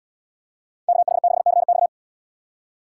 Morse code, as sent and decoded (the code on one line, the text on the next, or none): -.- .... -.... -.-. -..-
KH6CX